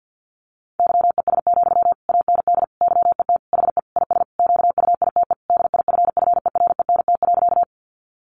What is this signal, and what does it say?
Morse code, heard as a keyed tone, about 50 wpm, 700 Hz. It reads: YES, AND YET HE IS QUITE DIFFERENT.
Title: Sentences from Top 700 Words 50wpm